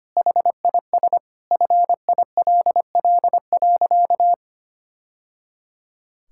Morse code, heard as a keyed tone, about 25 wpm, 700 Hz. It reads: HIS FILL.
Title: Sentences from Top 1000 Words - Review Part 5 - 25wpm